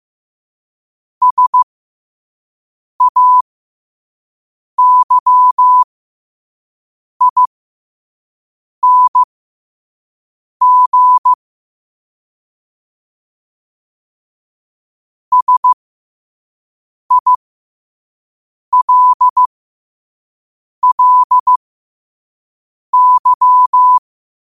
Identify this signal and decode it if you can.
Morse code, heard as a keyed tone, about 15 wpm, spaced as for 5 wpm, 1000 Hz. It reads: SAYING SILLY
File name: qotdfiles / 5wpm